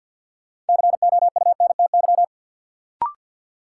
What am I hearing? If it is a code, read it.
Morse code, heard as a keyed tone, about 53 wpm, 700 Hz. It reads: COUNTY
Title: Sentences from Top 1200 Words - New Word Drill Alt - 50wpm